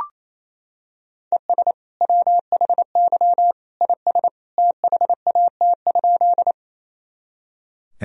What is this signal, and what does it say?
Morse code, heard as a keyed tone, about 28 wpm, 700 Hz. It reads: ES WHY IS THAT?
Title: Sentences from Top 200 Words and Abbreviations 28wpm